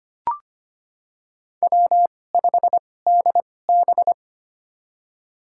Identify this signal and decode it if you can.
Morse code, heard as a keyed tone, about 25 wpm, spaced as for 18 wpm, 700 Hz. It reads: W5DB
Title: USA Call Signs - 4 Characters - 2x Character Spacing - 25wpm